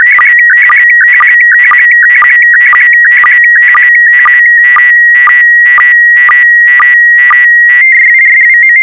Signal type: SSTV (slow-scan television)